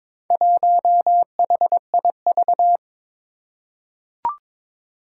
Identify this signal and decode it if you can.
Morse code, heard as a keyed tone, about 22 wpm, 700 Hz. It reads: 1HIV
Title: Four Challenging Letter-Number Combinations 22wpm